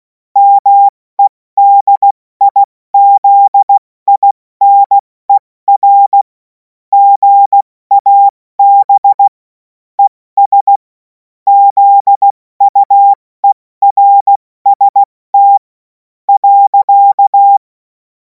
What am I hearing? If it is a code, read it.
Morse code, heard as a keyed tone, about 16 wpm, spaced as for 14 wpm, 800 Hz. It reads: MEDIZINER GAB ES ZUERST.
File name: Morsekurs / Witze / Witz 24. Februar 2026 70 F80